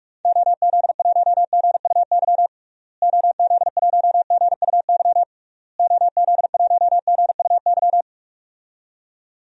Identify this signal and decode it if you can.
Morse code, heard as a keyed tone, about 44 wpm, 700 Hz. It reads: OZ1GUY OZ1GUY OZ1GUY
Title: DXCC-100 Call Signs - 6 Characters - Repeated 3x - 45wpm